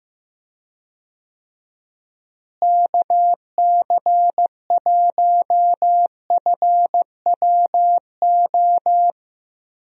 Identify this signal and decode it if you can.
Morse code, heard as a keyed tone, about 15 wpm, 700 Hz. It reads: KC1FWO
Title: USA Call Signs - 6 Characters - Repeated 3x - 15wpm